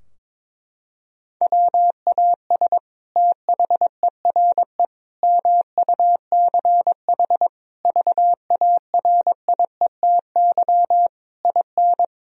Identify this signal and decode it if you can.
Morse code, heard as a keyed tone, about 22 wpm, 700 Hz. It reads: WAS THERE MUCH VARIETY IN
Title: Sentences from Top 1000 Words - Review Part 4 - 22wpm